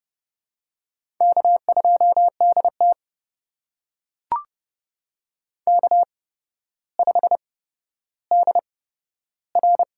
Morse code, heard as a keyed tone, about 32 wpm, spaced as for 13 wpm, 700 Hz. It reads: K2DT K5DR
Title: USA Call Signs - 4 Characters - 8x Character Spacing - 30wpm